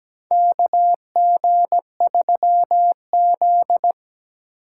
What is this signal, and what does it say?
Morse code, heard as a keyed tone, about 17 wpm, 700 Hz. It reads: KG3Z
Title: USA Call Signs - 4 Characters - 3x Character Spacing - 17wpm